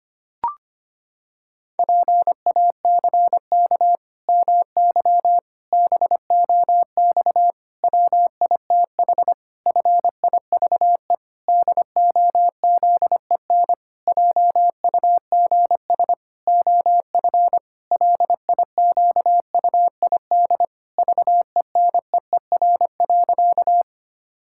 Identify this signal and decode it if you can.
Morse code, heard as a keyed tone, about 25 wpm, 700 Hz. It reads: PACK MY BOX WITH FIVE DOZEN JUGS OF LIQUID VENEER.
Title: Pangram Challenge v2 - 25wpm